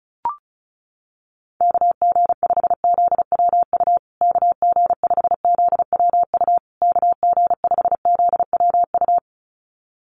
Morse code, heard as a keyed tone, about 34 wpm, 700 Hz. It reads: KG5ZWU KG5ZWU KG5ZWU
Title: USA Call Signs Repeated 3x 35wpm